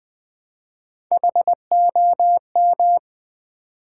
Morse code, heard as a keyed tone, about 20 wpm, 700 Hz. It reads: HOM
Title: Build A Word - 1 Letter At A Time - 3 to 5-Letter Words - 20wpm